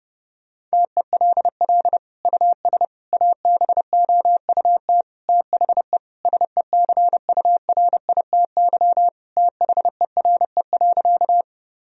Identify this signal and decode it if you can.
Morse code, heard as a keyed tone, about 32 wpm, 700 Hz. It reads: TELL US ABOUT THE SECURITY THERE.
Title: Sentences from Top 1000 Words - Review Part 4 - 30wpm